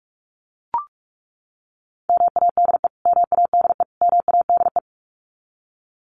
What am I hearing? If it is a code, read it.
Morse code, heard as a keyed tone, about 44 wpm, 700 Hz. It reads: MADE MADE MADE
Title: Top 200 Words Repeated 45wpm